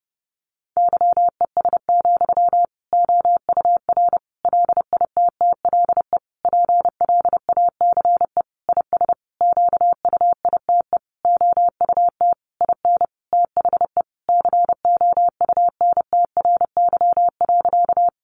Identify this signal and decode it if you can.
Morse code, heard as a keyed tone, about 30 wpm, 700 Hz. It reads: YES, OUR LITTLE PLACE IS QUITE OUT IN THE COUNTRY.
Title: Sentences from Top 700 Words 30wpm